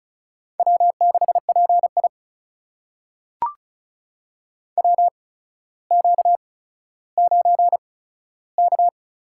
Morse code, heard as a keyed tone, about 36 wpm, spaced as for 15 wpm, 700 Hz. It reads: W6PI WQ9K
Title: USA Call Signs - 4 Characters - 8x Character Spacing - 35wpm